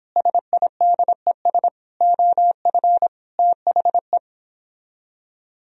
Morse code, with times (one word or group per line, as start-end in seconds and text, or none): 0.2-1.7 SIDES
2.0-3.1 OF
3.4-4.2 THE